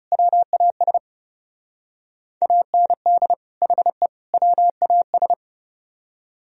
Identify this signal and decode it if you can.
Morse code, heard as a keyed tone, about 35 wpm, 700 Hz. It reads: WAS AND HE WAS
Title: Speed-Racing - Sets of 3 Words 30wpm